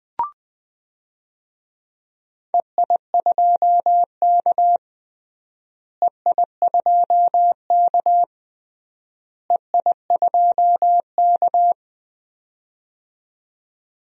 Morse code, heard as a keyed tone, about 19 wpm, 700 Hz. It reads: EI2K EI2K EI2K
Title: DXCC-100 Call Signs - 4 Characters - Repeated 3x - 20wpm